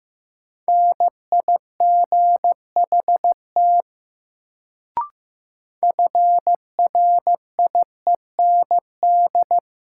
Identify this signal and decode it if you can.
Morse code, heard as a keyed tone, about 15 wpm, 700 Hz. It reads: NIGHT FRIEND
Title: Sentences from Top 300 Words - New Word Drill - 15wpm